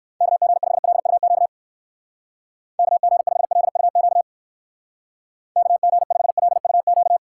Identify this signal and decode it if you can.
Morse code, heard as a keyed tone, about 64 wpm, 700 Hz. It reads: KG5LUQ KG5LUQ KG5LUQ